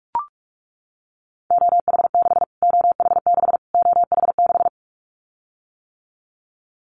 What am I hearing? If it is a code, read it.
Morse code, heard as a keyed tone, about 44 wpm, 700 Hz. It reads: OH6 OH6 OH6
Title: Callsign Prefixes for Top 25 DXCC - Repeated 3x - 45wpm